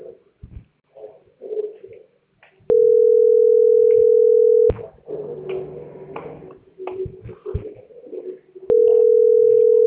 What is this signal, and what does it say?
Ringback tone (North America), 440 + 480 Hz.